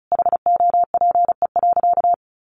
Morse code, heard as a keyed tone, about 34 wpm, 700 Hz. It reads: HOPE.
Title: Sentences from Top 1100 Words 35wpm